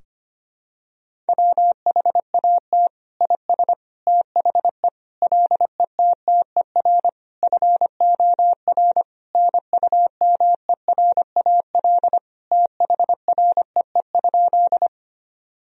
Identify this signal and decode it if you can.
Morse code, heard as a keyed tone, about 25 wpm, 700 Hz. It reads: WHAT IS THE LETTER FOR NUMERAL THREE?
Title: Sentences from Top 1000 Words - Review Part 2 - 25wpm